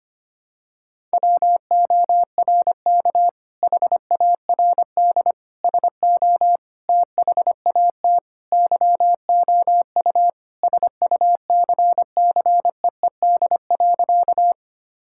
Morse code, heard as a keyed tone, about 25 wpm, 700 Hz. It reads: WORK HARD SO THAT YOU SUCCEED.
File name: practice / 25wpm / Sentences with 2-Word Phrases - 25wpm